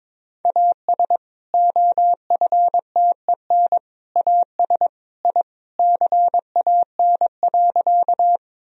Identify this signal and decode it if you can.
Morse code, heard as a keyed tone, about 22 wpm, 700 Hz. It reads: AS OFTEN AS I CAN.
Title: Sentences from Top 1000 Words - Review Part 1 - 22wpm